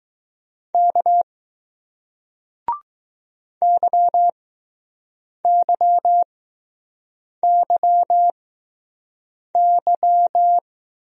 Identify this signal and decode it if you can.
Morse code, heard as a keyed tone, about 20 wpm, spaced as for 6 wpm, 700 Hz. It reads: K YYYY